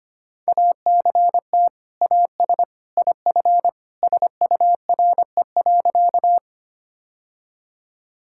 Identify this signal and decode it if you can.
Morse code, heard as a keyed tone, about 25 wpm, 700 Hz. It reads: ACT AS IF SURE.
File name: practice / 25wpm / Sentences with 2-Word Phrases - 25wpm